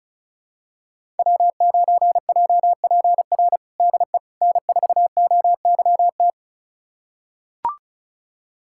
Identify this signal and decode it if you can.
Morse code, heard as a keyed tone, about 36 wpm, 700 Hz. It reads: W9JPR DE N4OYT
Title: USA Callsign DE Callsign - 5 Characters - 35wpm